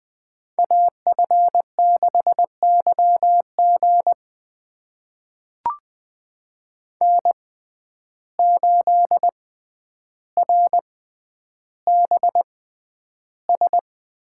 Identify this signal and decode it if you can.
Morse code, heard as a keyed tone, about 20 wpm, spaced as for 10 wpm, 700 Hz. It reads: AF6YG N8RBS
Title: USA Call Signs - 5 Characters - 6x Character Spacing - 20wpm